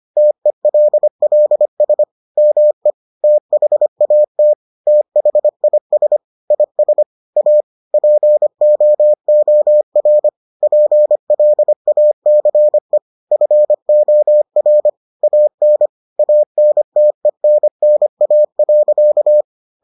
Morse code, heard as a keyed tone, about 25 wpm, 600 Hz. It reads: TELLS ME THAT THIS IS A POOR PLACE FOR AN ANTENNA.